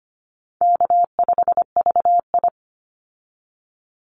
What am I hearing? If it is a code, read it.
Morse code, heard as a keyed tone, about 25 wpm, 700 Hz. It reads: K5VI
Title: Call Signs Repeated 2x - 25wpm